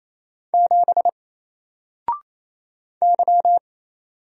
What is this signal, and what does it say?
Morse code, heard as a keyed tone, about 28 wpm, 700 Hz. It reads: 7 Y
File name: Single-Letter-Number-28wpm